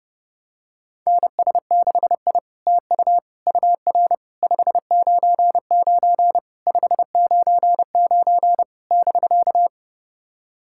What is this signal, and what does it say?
Morse code, heard as a keyed tone, about 30 wpm, 700 Hz. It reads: NS6I TU UR 599 599 <BK>
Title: SOTA QSO Practice 30wpm